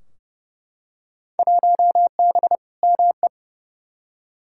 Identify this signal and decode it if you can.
Morse code, heard as a keyed tone, about 30 wpm, 700 Hz. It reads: 1B ME
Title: Field Day Exchanges - 30wpm